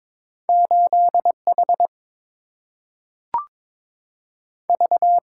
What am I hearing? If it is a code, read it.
Morse code, heard as a keyed tone, about 22 wpm, 690 Hz. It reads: 8 H V
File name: practice / 22wpm / Two Letter-Number Combinations - 22wpm